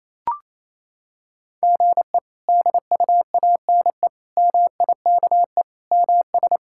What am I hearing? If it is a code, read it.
Morse code, heard as a keyed tone, about 28 wpm, 700 Hz. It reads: GE DUANE MIKE MS
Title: K1USN SST QSO Practice - 28wpm